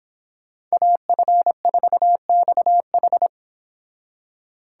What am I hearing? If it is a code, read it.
Morse code, heard as a keyed tone, about 29 wpm, 700 Hz. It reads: AF4XH